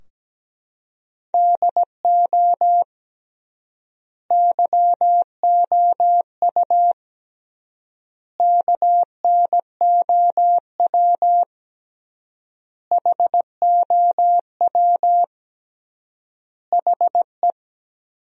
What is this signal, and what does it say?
Morse code, heard as a keyed tone, about 17 wpm, 700 Hz. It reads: DO YOU KNOW HOW HE
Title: 3x Word-Spacing - Sentences from Top 300 Words - 17wpm